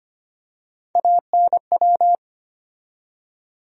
Morse code, heard as a keyed tone, about 25 wpm, 700 Hz. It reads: ANW
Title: USA Call Sign Postfixes - 3 Letters 25wpm